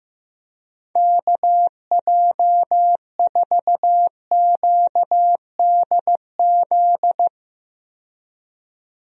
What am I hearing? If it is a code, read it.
Morse code, heard as a keyed tone, about 15 wpm, 700 Hz. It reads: KJ4QDZ
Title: USA Call Signs - 6 Characters - 15wpm